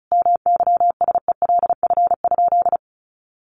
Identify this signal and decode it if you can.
Morse code, heard as a keyed tone, about 34 wpm, 700 Hz. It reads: MYSELF?
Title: Sentences from Top 1300 Words 35wpm